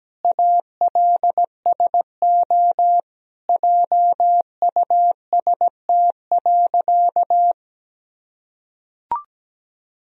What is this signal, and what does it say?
Morse code, heard as a keyed tone, about 17 wpm, 700 Hz. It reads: ALSO JUST.
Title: Sentences from Top 1000 Words - Review Part 1 - 17wpm